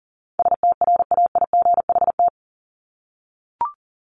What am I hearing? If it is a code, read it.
Morse code, heard as a keyed tone, about 40 wpm, 700 Hz. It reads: STRAIGHT
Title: Top 751-1000 Words 40wpm